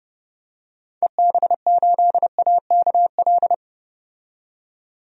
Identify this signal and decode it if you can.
Morse code, heard as a keyed tone, about 31 wpm, 700 Hz. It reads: EB8AKL
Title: DXCC-100 Call Signs - 6 Characters - 30wpm